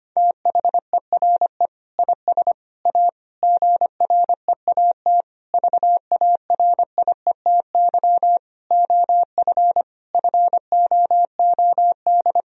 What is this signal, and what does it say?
Morse code, heard as a keyed tone, about 25 wpm, 700 Hz. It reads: THERE IS A GREAT VARIETY OF FOOD
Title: Sentences from Top 1000 Words - Review Part 4 - 25wpm